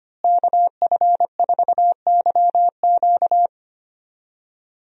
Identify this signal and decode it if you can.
Morse code, heard as a keyed tone, about 25 wpm, 700 Hz. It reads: KF4YQ
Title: USA Call Signs - 5 Characters - Repeated 2x - 25wpm